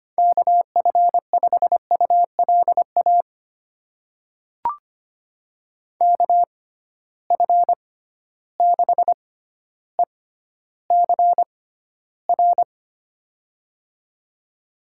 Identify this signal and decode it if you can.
Morse code, heard as a keyed tone, about 25 wpm, spaced as for 13 wpm, 700 Hz. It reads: KF5ULA KF6ECR
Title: USA Call Signs - 6 Characters - 6x Character Spacing - 25wpm